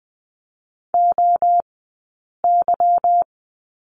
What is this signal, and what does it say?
Morse code, heard as a keyed tone, about 20 wpm, 700 Hz. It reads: O Y
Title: 190 Lesson 31d - Letter v - Two Characters 20wpm